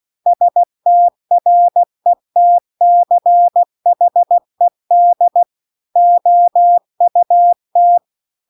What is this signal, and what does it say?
Morse code, heard as a keyed tone, about 16 wpm, 700 Hz. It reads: STRETCHED OUT